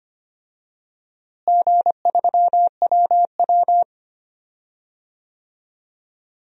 Morse code, heard as a keyed tone, about 25 wpm, 700 Hz. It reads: G3WW
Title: DXCC-100 Call Signs - 4 Characters - Repeated 3x - 25wpm